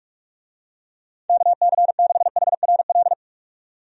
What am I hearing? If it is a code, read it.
Morse code, heard as a keyed tone, about 46 wpm, 700 Hz. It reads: KC6HRL
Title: USA Call Signs - 6 Characters - 45wpm